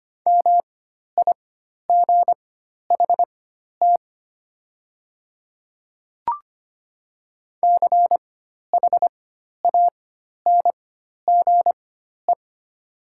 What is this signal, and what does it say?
Morse code, heard as a keyed tone, about 25 wpm, spaced as for 12 wpm, 700 Hz. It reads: MIGHT CHANGE
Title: 4x Character Spacing - Top 300 Words - 25wpm